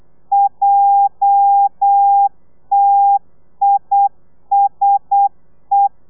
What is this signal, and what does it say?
Morse code, heard as a keyed tone, about 8 wpm, 800 Hz. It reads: JTISE
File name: telegrafia / i2tfj / Lez11 35